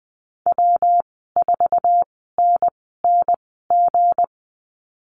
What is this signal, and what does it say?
Morse code, heard as a keyed tone, about 20 wpm, spaced as for 15 wpm, 700 Hz. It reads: W4NNG
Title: 193 Lesson 31g - Letter v - Call Signs 20wpm